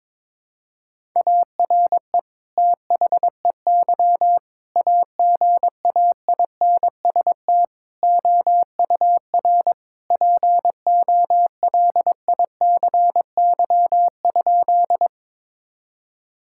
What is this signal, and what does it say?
Morse code, heard as a keyed tone, about 22 wpm, 700 Hz. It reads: ARE THEY AGAINST OUR POLICY?
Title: Sentences from Top 1000 Words - Review Part 5 - 22wpm